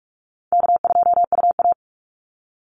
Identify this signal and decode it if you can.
Morse code, heard as a keyed tone, about 44 wpm, 700 Hz. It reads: K2UA
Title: Call Signs Repeated 3x - 45wpm